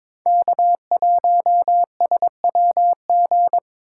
Morse code, heard as a keyed tone, about 22 wpm, 700 Hz. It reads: K1SWG
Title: USA Call Signs - 4x Character Spacing - 22wpm